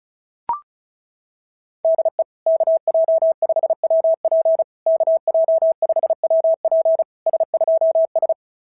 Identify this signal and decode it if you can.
Morse code, heard as a keyed tone, about 36 wpm, 660 Hz. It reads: DE KJ5WP KJ5WP S2S
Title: SOTA QSO Practice 35wpm